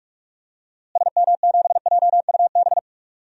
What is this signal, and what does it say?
Morse code, heard as a keyed tone, about 46 wpm, 700 Hz. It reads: SM7JUB
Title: DXCC-100 Call Signs - 6 Characters - Repeated 2x - 45wpm